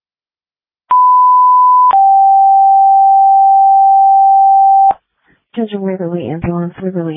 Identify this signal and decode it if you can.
Two-tone page, heard as a one-second tone, then a three-second tone, about 1000 then 780 Hz, with speech. It is heard from 0.9 to 4.9 s.
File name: Waverly Ambulance 2026 06 08 20 36 52